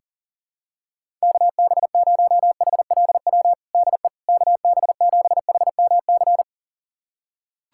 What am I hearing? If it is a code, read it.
Morse code, heard as a keyed tone, about 40 wpm, 700 Hz. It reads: KB0HLW DE KB7HMC
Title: USA Callsign DE Callsign - 6 Characters - 40wpm